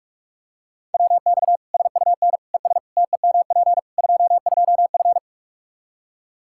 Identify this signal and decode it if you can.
Morse code, heard as a keyed tone, about 42 wpm, 700 Hz. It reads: WX SUN ES TEMP 22F
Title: Practice QSO - Exchange 2 - Info - 45wpm